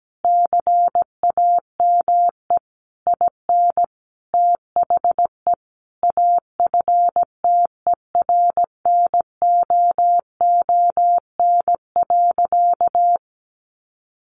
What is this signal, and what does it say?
Morse code, heard as a keyed tone, about 17 wpm, 700 Hz. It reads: CAME IN THE AFTERNOON.
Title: Sentences from Top 1500 Words 17wpm